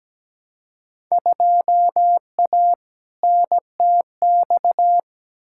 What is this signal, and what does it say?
Morse code, heard as a keyed tone, about 17 wpm, 700 Hz. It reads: 2A NTX
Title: Field Day Exchanges - 17wpm